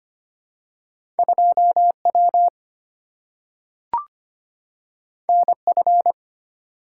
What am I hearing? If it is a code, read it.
Morse code, heard as a keyed tone, about 25 wpm, 700 Hz. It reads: 2W NF
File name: Two Letter-Number Combinations - 25wpm